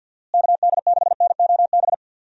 Morse code, heard as a keyed tone, about 50 wpm, 700 Hz. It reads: KD6NOB
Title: USA Call Signs - 6 Characters - Repeated 3x - 50wpm